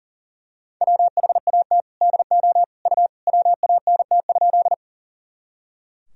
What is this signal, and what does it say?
Morse code, heard as a keyed tone, about 40 wpm, 700 Hz. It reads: WHAT DO U WANT?
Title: Sentences from Top 200 Words and Abbreviations 40wpm